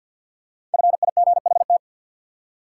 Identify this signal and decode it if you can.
Morse code, heard as a keyed tone, about 49 wpm, 700 Hz. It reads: FIGHT
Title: Five Letter Words - 50wpm